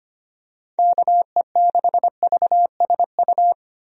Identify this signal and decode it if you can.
Morse code, heard as a keyed tone, about 25 wpm, 700 Hz. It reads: KE6VSU
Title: USA Call Signs - 6 Characters - 25wpm